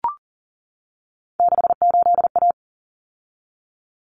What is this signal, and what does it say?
Morse code, heard as a keyed tone, about 40 wpm, 700 Hz. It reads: 68A
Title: Common Cut Numbers - 3-Digit 40wpm